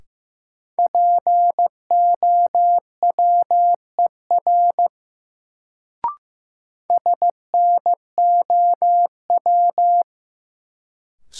Morse code, heard as a keyed tone, about 15 wpm, 700 Hz. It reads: POWER SNOW
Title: Sentences from Top 500 Words - New Word Drill - 15wpm